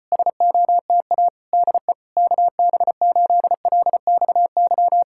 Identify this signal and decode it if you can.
Morse code, heard as a keyed tone, about 35 wpm, 700 Hz. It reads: SOTA DE KB8LXY